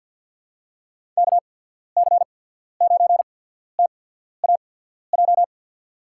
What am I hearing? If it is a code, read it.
Morse code, heard as a keyed tone, about 51 wpm, spaced as for 14 wpm, 700 Hz. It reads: KC9TAJ
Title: USA Call Signs - 6 Characters - 8x Character Spacing - 50wpm